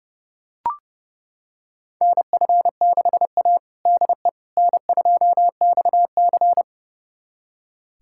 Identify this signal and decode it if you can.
Morse code, heard as a keyed tone, about 30 wpm, 700 Hz. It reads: NF6A DE N2XC
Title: USA Callsign DE Callsign - 4 Characters - 30wpm